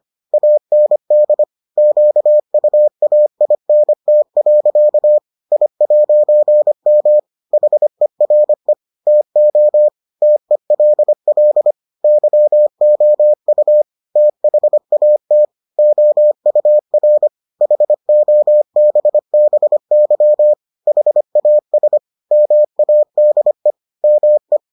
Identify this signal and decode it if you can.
Morse code, heard as a keyed tone, about 25 wpm, 600 Hz. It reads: AND QUAINT. I'M HERE TO TELL YOU THAT OUR HOBBY HAS MADE ME